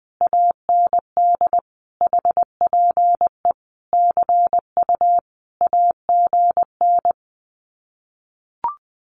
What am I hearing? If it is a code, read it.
Morse code, heard as a keyed tone, about 20 wpm, 700 Hz. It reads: AND HPE CU AGN
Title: Common QSO Phrases Encore 20wpm